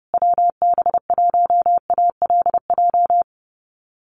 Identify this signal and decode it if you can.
Morse code, heard as a keyed tone, about 30 wpm, 700 Hz. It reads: WB1ALJ
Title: USA Call Signs Repeated 2x 30wpm